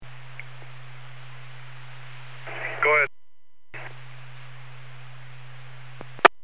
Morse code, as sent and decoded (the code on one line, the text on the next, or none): none